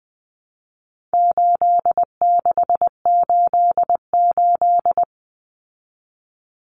Morse code, heard as a keyed tone, about 20 wpm, 700 Hz. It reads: 8688
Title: Numbers - 4-Digit 20wpm